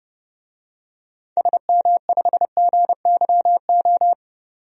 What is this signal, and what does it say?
Morse code, heard as a keyed tone, about 30 wpm, 700 Hz. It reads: SM5GYO
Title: DXCC-100 Call Signs - 6 Characters - 30wpm